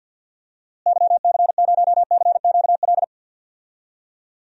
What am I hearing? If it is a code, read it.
Morse code, heard as a keyed tone, about 52 wpm, 700 Hz. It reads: YC0CQL
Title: DXCC-100 Call Signs - 6 Characters - Repeated 3x - 50wpm